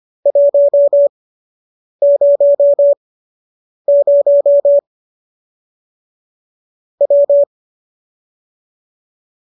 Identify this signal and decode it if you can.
Morse code, heard as a keyed tone, about 25 wpm, spaced as for 8 wpm, 570 Hz. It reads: 100 W